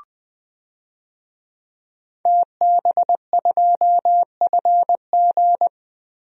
Morse code, heard as a keyed tone, about 20 wpm, 700 Hz. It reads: TB2FG
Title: DXCC-100 Call Signs - 5 Characters - Repeated 3x - 20wpm